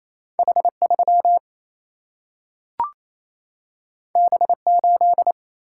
Morse code, heard as a keyed tone, about 28 wpm, 700 Hz. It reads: H3 B8
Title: Single-Letter Single-Number 28wpm